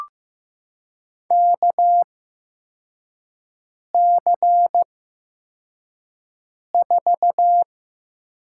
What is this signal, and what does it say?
Morse code, heard as a keyed tone, about 15 wpm, spaced as for 4 wpm, 700 Hz. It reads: KC4
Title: USA Call Signs - 6 Characters - 8x Character Spacing - 15wpm